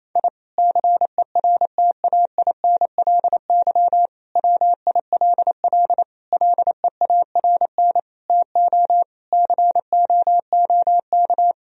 Morse code, heard as a keyed tone, about 27 wpm, 700 Hz. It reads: I CERTAINLY WILL LEARN TO COOK
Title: Sentences from Top 1000 Words - Review Part 4 - 28wpm